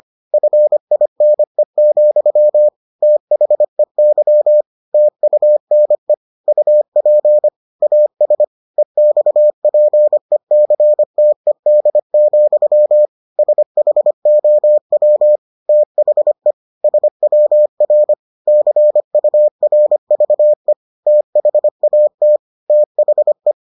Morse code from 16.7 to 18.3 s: SWR

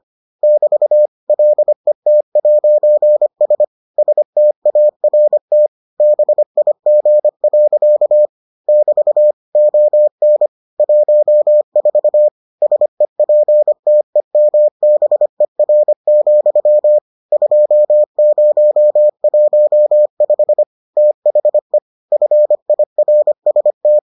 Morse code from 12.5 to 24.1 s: SEPTEMBER, 2015 THE FIRST

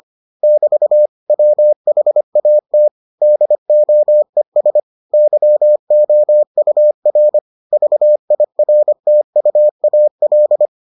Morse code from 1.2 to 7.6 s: WHAT DOES YOUR

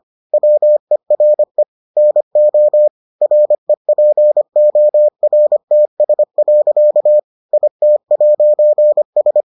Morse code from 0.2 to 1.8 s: WERE

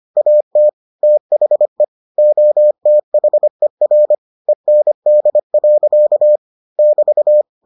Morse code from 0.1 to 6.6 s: AT THE OTHER END.

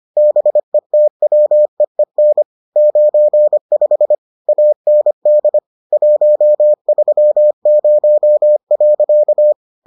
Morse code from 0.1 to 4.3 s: BETWEEN 95

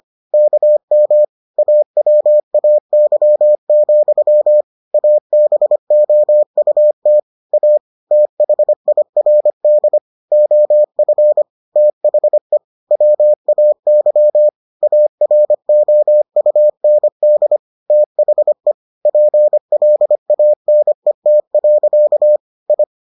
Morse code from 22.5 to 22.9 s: I

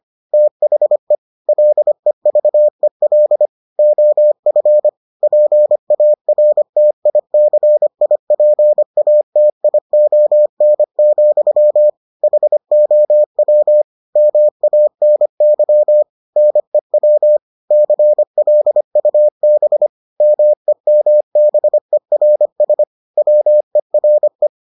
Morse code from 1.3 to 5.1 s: LEVEL OF

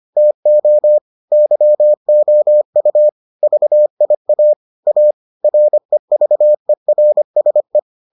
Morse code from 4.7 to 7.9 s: A REVERSE

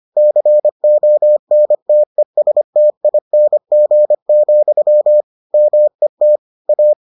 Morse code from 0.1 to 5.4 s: CONTESTING,

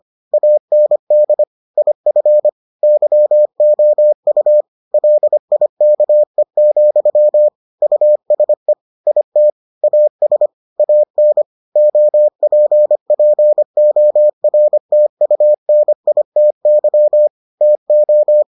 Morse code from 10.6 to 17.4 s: AN OPPORTUNITY

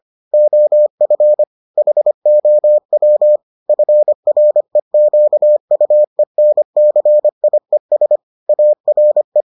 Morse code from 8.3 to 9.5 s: ARE

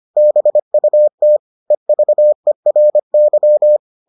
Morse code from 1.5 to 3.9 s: EVERY

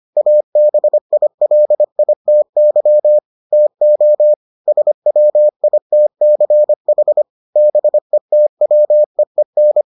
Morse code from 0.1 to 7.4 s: ABILITY TO SWITCH

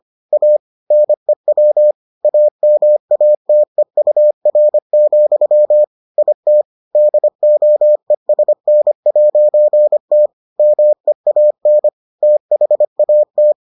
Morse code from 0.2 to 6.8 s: A NEW AMATEUR, IT

